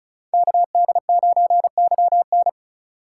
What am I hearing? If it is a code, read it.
Morse code, heard as a keyed tone, about 36 wpm, 700 Hz. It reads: KD9YN
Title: Speed Racing - USA Call Signs 30wpm